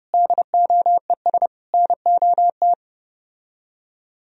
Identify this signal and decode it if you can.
Morse code, heard as a keyed tone, about 32 wpm, 700 Hz. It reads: DOES NOT